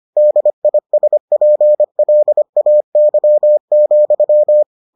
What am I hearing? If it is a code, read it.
Morse code, heard as a keyed tone, about 25 wpm, 600 Hz. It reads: DISPLAY,